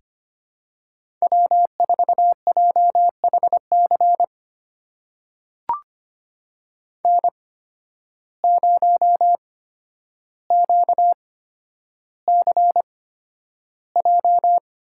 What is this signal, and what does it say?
Morse code, heard as a keyed tone, about 25 wpm, spaced as for 11 wpm, 700 Hz. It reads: W4JHC N0QCJ
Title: USA Call Signs - 5 Characters - 8x Character Spacing - 25wpm